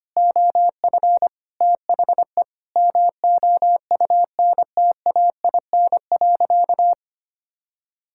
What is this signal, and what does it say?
Morse code, heard as a keyed tone, about 25 wpm, 700 Hz. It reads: OF THE MOUNTAIN.
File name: practice / 25wpm / Sentences with 3-Word Phrases - 25wpm